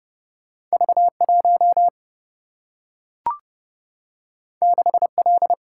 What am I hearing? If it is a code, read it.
Morse code, heard as a keyed tone, about 30 wpm, 700 Hz. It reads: V1 6L
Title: Single-Letter Single-Number 30wpm